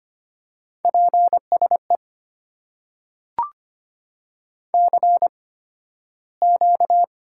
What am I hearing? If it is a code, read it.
Morse code, heard as a keyed tone, about 25 wpm, spaced as for 13 wpm, 700 Hz. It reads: PSE CQ
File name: Top 90 QSO Elements - 8x Character Spacing - 25wpm